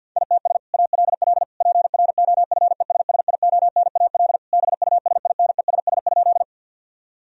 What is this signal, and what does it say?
Morse code, heard as a keyed tone, about 50 wpm, 700 Hz. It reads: ITS ALL PROFESSIONAL BUSINESS?
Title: Sentences from Top 1000 Words - Review Part 4 - 50wpm